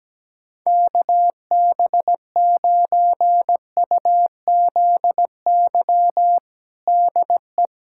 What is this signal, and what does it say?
Morse code, heard as a keyed tone, about 17 wpm, 700 Hz. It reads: KB9UZY DE
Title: USA Callsign DE Callsign - 6 Characters - 17wpm